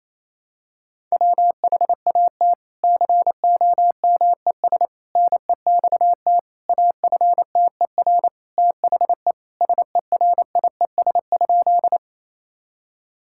Morse code, heard as a keyed tone, about 27 wpm, 700 Hz. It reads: WHAT COMES NEXT AFTER THE SERIES?
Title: Sentences from Top 1000 Words - Review Part 5 - 28wpm